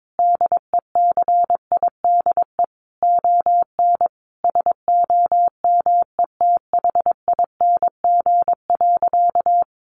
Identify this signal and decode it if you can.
Morse code, heard as a keyed tone, about 22 wpm, 700 Hz. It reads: DECIDE ON SOMETHING.
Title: Sentences from Top 600 Words 22wpm